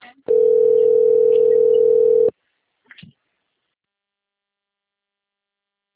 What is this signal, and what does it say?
Ringback tone (North America), 440 + 480 Hz.
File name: rg-1600-unknown- - - .85450